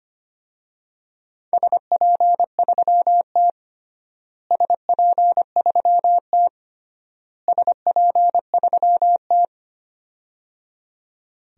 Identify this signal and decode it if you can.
Morse code, heard as a keyed tone, about 25 wpm, 700 Hz. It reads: SP3T SP3T SP3T
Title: DXCC-100 Call Signs - 4 Characters - Repeated 3x - 25wpm